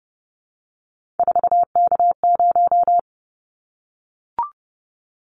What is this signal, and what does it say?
Morse code, heard as a keyed tone, about 29 wpm, 700 Hz. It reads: 4K0
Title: Callsign Prefixes for Top 100 DXCC - 30wpm